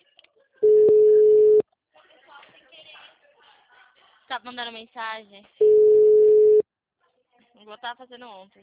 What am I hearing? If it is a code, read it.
Ringback tone (Europe, 425 Hz), with speech.